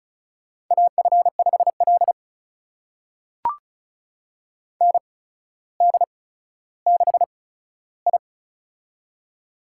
Morse code, heard as a keyed tone, about 36 wpm, spaced as for 15 wpm, 700 Hz. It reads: AF5L ND6I
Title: USA Call Signs - 8x Character Spacing - 35wpm